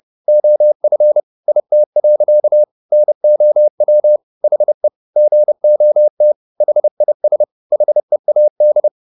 Morse code, heard as a keyed tone, about 30 wpm, 600 Hz. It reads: OF IT. NOW HE GOT HIS HEAD